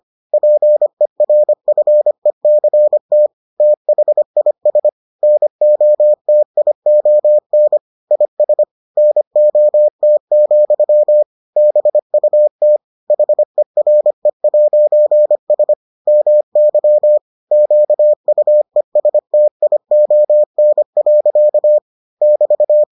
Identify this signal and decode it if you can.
Morse code, heard as a keyed tone, about 25 wpm, 600 Hz. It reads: PERFECT THIS NOTION IS NOT, BUT HERE'S MY QUESTION. <BT>